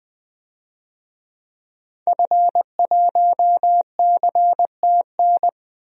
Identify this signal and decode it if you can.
Morse code, heard as a keyed tone, about 20 wpm, 700 Hz. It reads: F1CTN
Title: DXCC-100 Call Signs - 5 Characters - 20wpm